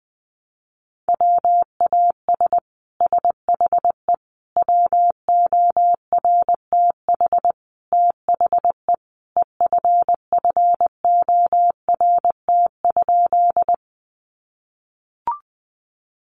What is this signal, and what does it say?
Morse code, heard as a keyed tone, about 20 wpm, 700 Hz. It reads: WAS SHE WORTH THE EFFORT?
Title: Sentences from Top 1200 Words 20wpm